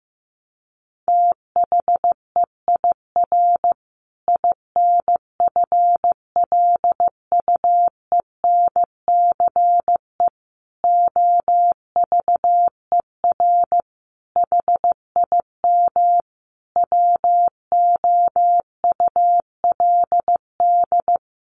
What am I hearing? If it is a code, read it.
Morse code, heard as a keyed tone, about 15 wpm, 700 Hz. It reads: THEIR INFLUENCE OVER HIM WOULD
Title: Sentences from Top 1400 Words 15wpm